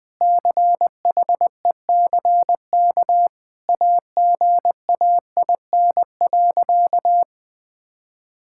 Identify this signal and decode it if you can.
Morse code, heard as a keyed tone, about 20 wpm, 700 Hz. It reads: CHECK AGAIN.
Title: Sentences with 4-Word Phrases - 20wpm